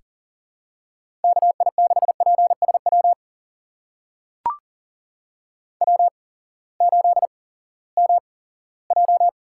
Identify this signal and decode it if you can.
Morse code, heard as a keyed tone, about 40 wpm, spaced as for 20 wpm, 700 Hz. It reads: KI6PSW W8MJ
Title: USA Call Signs - 8x Character Spacing - 40wpm